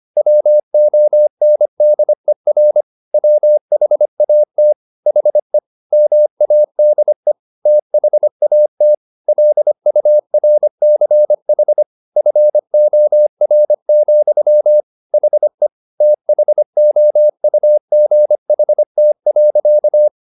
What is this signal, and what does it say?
Morse code, heard as a keyed tone, about 25 wpm, 600 Hz. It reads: WONDER WHAT HE MADE THAT LURCH FOR, HE THOUGHT.
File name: the old man and the sea part 1 25wpm